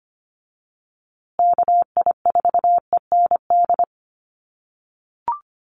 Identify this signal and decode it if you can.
Morse code, heard as a keyed tone, about 25 wpm, 700 Hz. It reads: KI4END
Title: USA Call Signs Repeated 3x 25wpm